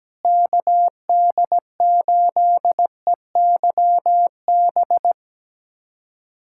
Morse code, heard as a keyed tone, about 17 wpm, 700 Hz. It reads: KD8EYB
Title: USA Callsign DE Callsign - 6 Characters - 17wpm